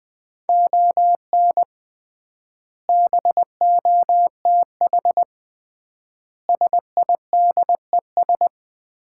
Morse code, heard as a keyed tone, about 20 wpm, 700 Hz. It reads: ON BOTH SIDES
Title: 3x Word-Spacing - Sets of 4 Words - 20wpm